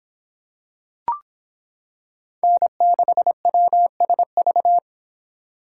Morse code, heard as a keyed tone, about 26 wpm, 700 Hz. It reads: N6WSV